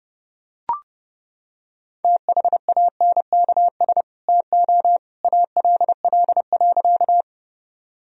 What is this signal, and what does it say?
Morse code, heard as a keyed tone, about 30 wpm, 700 Hz. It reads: THANKS TO ALL.
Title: Sentences from Top 1000 Words - Review Part 3 - 30wpm